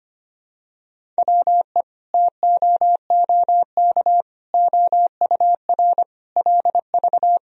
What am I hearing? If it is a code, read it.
Morse code, heard as a keyed tone, about 25 wpm, 700 Hz. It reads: WE TOOK OUR LV
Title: Sentences from Top 400 Words and Abbreviations 25wpm